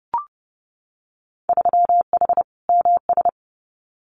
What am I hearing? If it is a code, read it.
Morse code, heard as a keyed tone, about 32 wpm, 700 Hz. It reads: 3H MS
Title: Winter Field Day Exchanges Encore 30wpm